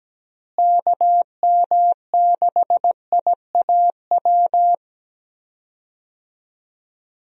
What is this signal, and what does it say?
Morse code, heard as a keyed tone, about 17 wpm, 700 Hz. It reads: KM6IAW
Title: USA Call Signs - 6 Characters - Repeated 2x - 17wpm